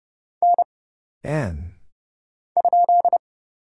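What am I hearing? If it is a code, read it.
Morse code, heard as a keyed tone, about 30 wpm, 700 Hz. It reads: N ?
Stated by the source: Morse Code Ninja single-character practice (letters, numbers, punctuation, prosigns) at 30 wpm, by Kurt Zoglmann (AD0WE).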